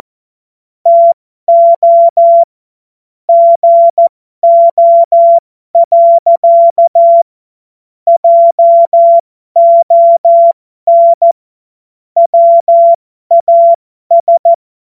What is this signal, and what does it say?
Morse code, heard as a keyed tone, about 14 wpm, spaced as for 12 wpm, 680 Hz. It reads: TO GO. JON WAS
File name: Morsekurs / Witze